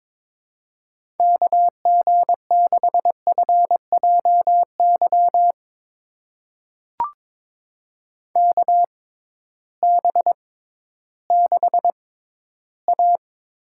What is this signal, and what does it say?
Morse code, heard as a keyed tone, about 22 wpm, spaced as for 13 wpm, 700 Hz. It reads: KG6FJY KB6A
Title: USA Call Signs - 6 Characters - 6x Character Spacing - 22wpm